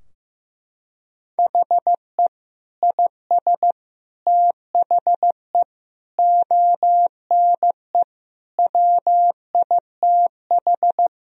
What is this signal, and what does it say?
Morse code, heard as a keyed tone, about 15 wpm, 700 Hz. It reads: HE IS THE ONE WITH
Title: Sentences from Top 1000 Words - Review Part 3 - 15wpm